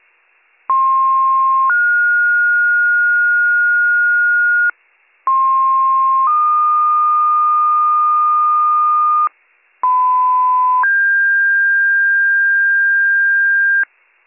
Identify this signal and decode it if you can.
Two-tone page, heard as a one-second tone, then a three-second tone, about 1060 then 1430 Hz, about 1060 then 1220 Hz, about 1010 then 1600 Hz, with noise. It is heard from 0.7 to 13.8 s.